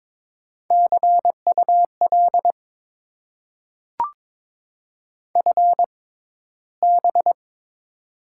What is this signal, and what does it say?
Morse code, heard as a keyed tone, about 22 wpm, spaced as for 13 wpm, 700 Hz. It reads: CUL FB
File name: Top 90 QSO Elements - 6x Character Spacing - 22wpm